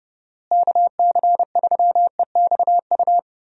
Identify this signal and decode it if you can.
Morse code, heard as a keyed tone, about 32 wpm, 700 Hz. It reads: KC3EXU
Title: USA Call Signs - 6 Characters - Repeated 2x - 30wpm